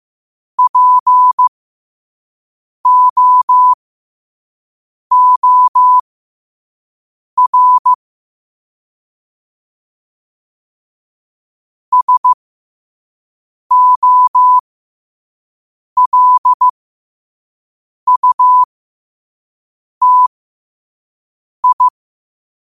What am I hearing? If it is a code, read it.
Morse code, heard as a keyed tone, about 15 wpm, spaced as for 5 wpm, 1000 Hz. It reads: POOR SOLUTI
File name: 5wpm